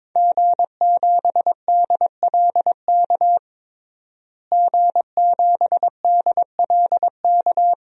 Morse code, heard as a keyed tone, about 22 wpm, 700 Hz. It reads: G7DLK G7DLK
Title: DXCC-100 Call Signs - 5 Characters - Repeated 2x - 22wpm